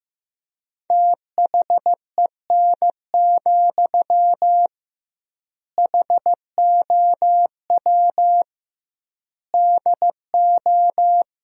Morse code, heard as a keyed tone, about 15 wpm, 700 Hz. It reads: THEN, HOW DO
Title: 2x Word-Spacing - Sentences from Top 100 Words - 15wpm